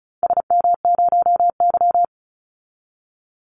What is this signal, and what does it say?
Morse code, heard as a keyed tone, about 36 wpm, 700 Hz. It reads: SM0Y
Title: Call Signs - 35wpm